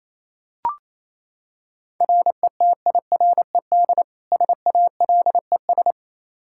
Morse code, heard as a keyed tone, about 28 wpm, 700 Hz. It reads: RETIRED SALES
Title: Practice QSO - Exchange 3 - More Info - 28wpm